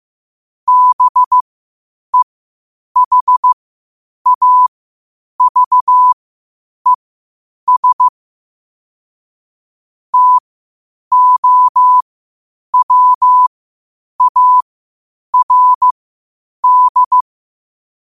Morse code, heard as a keyed tone, about 15 wpm, spaced as for 8 wpm, 1000 Hz. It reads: BEHAVES TOWARD
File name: qotdfiles / 8wpm